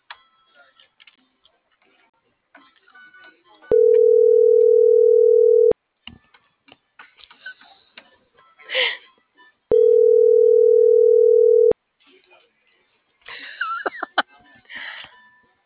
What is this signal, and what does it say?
Ringback tone (North America), 440 + 480 Hz.
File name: CENTRAL NMP / out itzel.arias